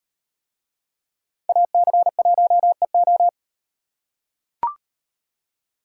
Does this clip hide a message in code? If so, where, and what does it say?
Yes: Morse code from 1.5 to 3.3 s, heard as a keyed tone, about 36 wpm, 700 Hz. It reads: AC1EO